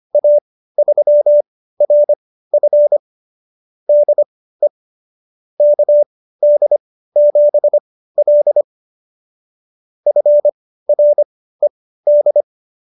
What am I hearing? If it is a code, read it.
Morse code, heard as a keyed tone, about 25 wpm, spaced as for 15 wpm, 590 Hz. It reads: A3RF DE KD7L FRED
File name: qso222 15